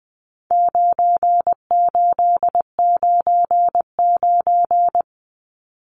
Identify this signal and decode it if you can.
Morse code, heard as a keyed tone, about 19 wpm, 700 Hz. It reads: 9899
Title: Numbers - 4-Digit 20wpm